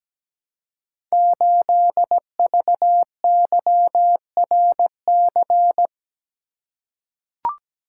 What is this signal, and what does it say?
Morse code, heard as a keyed tone, about 17 wpm, 700 Hz. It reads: 8VYRC
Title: Code Groups - 5 Letter:Numbers - 17wpm